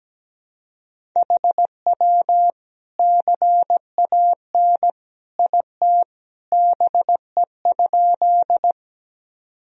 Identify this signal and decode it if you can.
Morse code, heard as a keyed tone, about 17 wpm, 700 Hz. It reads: HW CAN IT BE?
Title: Sentences from Top 100 Words and Abbreviations 17wpm